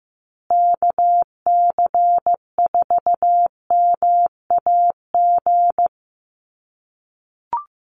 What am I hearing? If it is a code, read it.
Morse code, heard as a keyed tone, about 15 wpm, 700 Hz. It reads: KC4MAG
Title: USA Call Signs Repeated 3x 15wpm